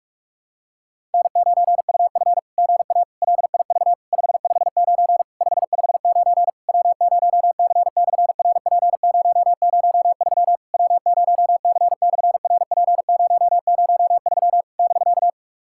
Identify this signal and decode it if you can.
Morse code, heard as a keyed tone, about 44 wpm, 700 Hz. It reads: N9UF GA LIV 559 559 W0C/RP003 W0C/RP003 <BK>
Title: SOTA QSO Practice 45wpm